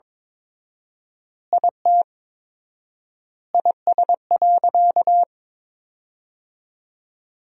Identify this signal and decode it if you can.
Morse code, heard as a keyed tone, about 22 wpm, 700 Hz. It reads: IT IS.
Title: 4x Word-Spacing - Sentences from Top 100 Words - 22wpm